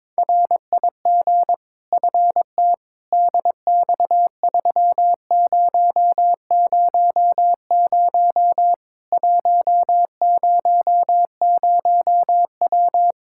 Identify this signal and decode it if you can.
Morse code, heard as a keyed tone, about 22 wpm, 700 Hz. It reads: RIG FT DX3000 100W
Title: K1USN SST QSO Practice 22wpm